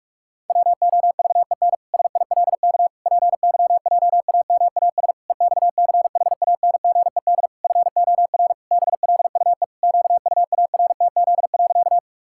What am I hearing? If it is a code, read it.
Morse code, heard as a keyed tone, about 46 wpm, 700 Hz. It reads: WOVEN SILK PYJAMAS EXCHANGED FOR BLUE QUARTZ.
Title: Pangram Challenge v2 - 45wpm